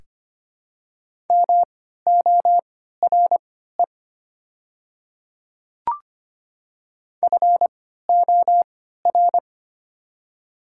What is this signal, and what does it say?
Morse code, heard as a keyed tone, about 25 wpm, spaced as for 14 wpm, 700 Hz. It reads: MORE FOR
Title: 3x Character Spacing - Top 100 Words - 25wpm